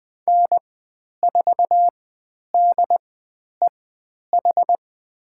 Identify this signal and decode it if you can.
Morse code, heard as a keyed tone, about 20 wpm, spaced as for 10 wpm, 700 Hz. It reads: N4DEH
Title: 095 Lesson 16g - Letter c - Call Signs 20wpm